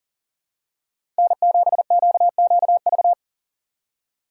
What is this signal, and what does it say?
Morse code, heard as a keyed tone, about 40 wpm, 700 Hz. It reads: N7QQV
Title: USA Call Signs - 5 Characters - 6x Character Spacing - 40wpm